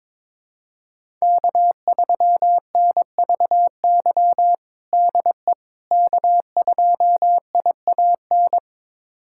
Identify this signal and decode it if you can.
Morse code, heard as a keyed tone, about 21 wpm, 700 Hz. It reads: K3NVY DE K2IAN
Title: USA Callsign DE Callsign - 5 Characters - 22wpm